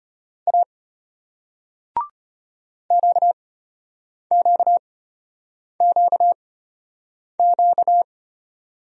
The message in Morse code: A QQQQ

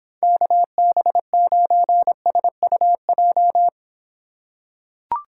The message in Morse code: KB9SUJ

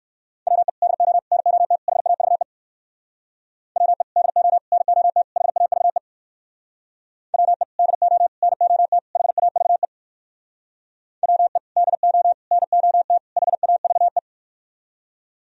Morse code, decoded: WE DO NOT HAVE WE DO NOT HAVE WE DO NOT HAVE WE DO NOT HAVE

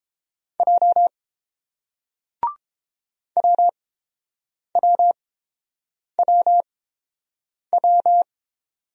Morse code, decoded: J WWWW

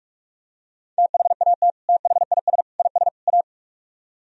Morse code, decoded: THAT THIS IS A

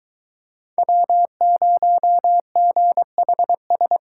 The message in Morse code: W0GHS

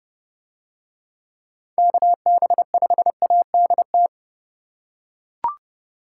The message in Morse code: KB5ADT

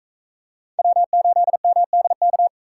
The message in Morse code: W8MDK

